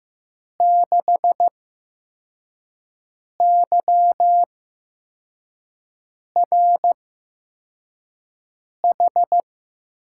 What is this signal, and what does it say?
Morse code, heard as a keyed tone, about 15 wpm, spaced as for 4 wpm, 700 Hz. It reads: 6YRH
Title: USA Call Signs - 6 Characters - 8x Character Spacing - 15wpm